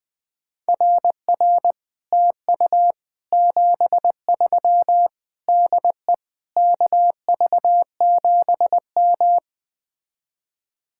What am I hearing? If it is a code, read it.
Morse code, heard as a keyed tone, about 20 wpm, 700 Hz. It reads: RR TU 73 DE KV7M